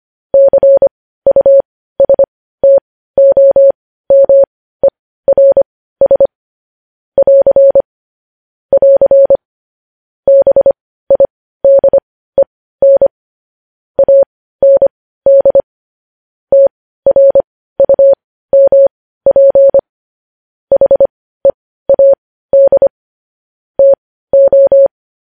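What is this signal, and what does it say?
Morse code, heard as a keyed tone, about 25 wpm, spaced as for 15 wpm, 550 Hz. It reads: CUSTOMERS <AR> <AR> BIDEN AND TRUMP HEAD TO
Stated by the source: Morse news headlines sent at 15 words per minute.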